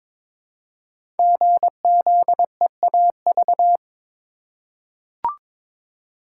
Morse code, heard as a keyed tone, about 23 wpm, 700 Hz. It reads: GZEAV
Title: Code Groups - 5 Letter:Numbers - 22wpm